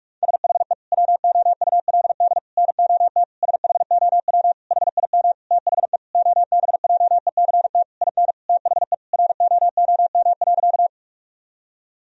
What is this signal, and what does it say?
Morse code, heard as a keyed tone, about 41 wpm, 700 Hz. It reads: SHE WOULD NOT SHOW HIM THE OBJECT IN THE ROOM.
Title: Sentences from Top 1000 Words - Review Part 3 - 45wpm